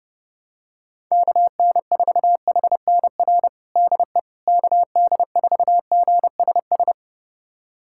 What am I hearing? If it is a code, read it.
Morse code, heard as a keyed tone, about 30 wpm, 700 Hz. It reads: KN4HNR DE KD4GSS
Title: USA Callsign DE Callsign - 6 Characters - 30wpm